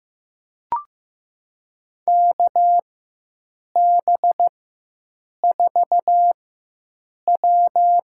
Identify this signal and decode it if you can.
Morse code, heard as a keyed tone, about 16 wpm, spaced as for 7 wpm, 700 Hz. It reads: KB4W
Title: USA Call Signs - 4x Character Spacing - 15wpm